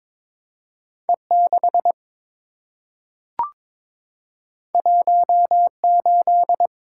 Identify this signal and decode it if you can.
Morse code, heard as a keyed tone, about 22 wpm, 700 Hz. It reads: E6 18
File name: practice / 22wpm / Two Letter-Number Combinations - 22wpm